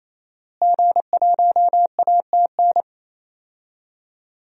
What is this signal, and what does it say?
Morse code, heard as a keyed tone, about 28 wpm, 700 Hz. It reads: G1ATN
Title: DXCC-100 Call Signs - 5 Characters - Repeated 3x - 28wpm